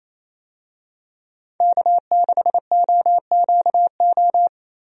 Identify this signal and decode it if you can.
Morse code, heard as a keyed tone, about 28 wpm, 700 Hz. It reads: K6OQO
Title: USA Call Signs - 5 Characters - Repeated 3x - 28wpm